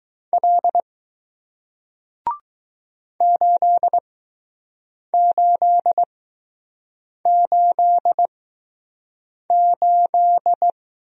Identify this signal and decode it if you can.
Morse code, heard as a keyed tone, about 20 wpm, spaced as for 6 wpm, 700 Hz. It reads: L 8888